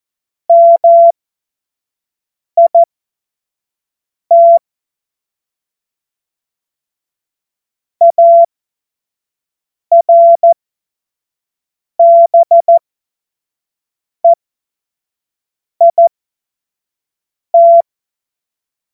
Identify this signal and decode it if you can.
Morse code, heard as a keyed tone, about 14 wpm, spaced as for 5 wpm, 680 Hz. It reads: MIT ARBEIT
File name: Witz-08-Juli-2025-25-F70